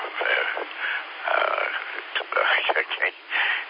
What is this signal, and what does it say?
Emergency Alert System (SAME) tones, heard as harsh buzzing bursts, with speech. It is heard from 0.7 to 3.2 s.